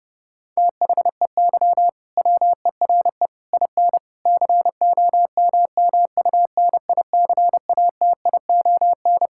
Morse code, heard as a keyed tone, about 30 wpm, 700 Hz. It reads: THEY WERE IN COMMUNICATION
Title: Sentences from Top 1000 Words - Review Part 6 - 30wpm